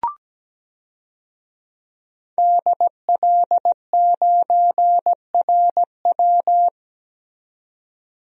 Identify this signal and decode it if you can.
Morse code, heard as a keyed tone, about 17 wpm, 700 Hz. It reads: DL9RW
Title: DXCC-100 Call Signs - 5 Characters - 17wpm